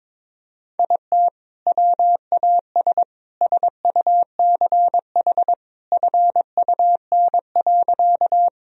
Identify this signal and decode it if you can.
Morse code, heard as a keyed tone, about 21 wpm, 700 Hz. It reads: IT WAS SUCH FUN.
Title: Sentences from Top 1000 Words - Review Part 4 - 22wpm